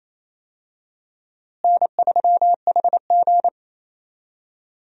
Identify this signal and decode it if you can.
Morse code, heard as a keyed tone, about 27 wpm, 700 Hz. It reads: N3HG
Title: USA Call Signs - 4 Characters - Repeated 3x - 28wpm